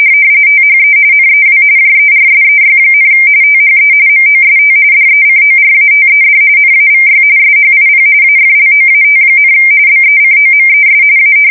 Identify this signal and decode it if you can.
RTTY, about 45.45 baud. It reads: THE NEW JERSEY, THE MOST DECORATED NAVY SHIP AFLOAT, BECOMES A
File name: fsk170 / battleshipnj